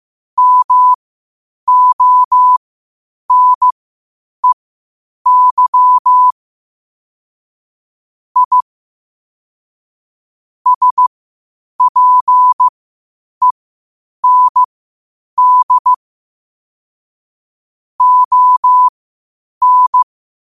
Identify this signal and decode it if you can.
Morse code, heard as a keyed tone, about 15 wpm, spaced as for 8 wpm, 1000 Hz. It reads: MONEY I SPEND ON
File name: qotdfiles / 8wpm